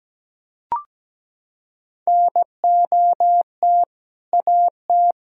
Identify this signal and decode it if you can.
Morse code, heard as a keyed tone, about 17 wpm, 700 Hz. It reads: NOT AT
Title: Sentences from Top 1000 Words - Review Part 4 - 17wpm